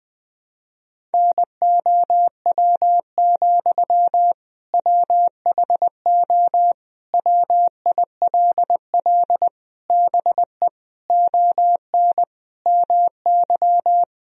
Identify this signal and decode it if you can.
Morse code, heard as a keyed tone, about 20 wpm, 700 Hz. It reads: NOW, WHO WILL BE ON MY
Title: Sentences from Top 1000 Words - Review Part 1 - 20wpm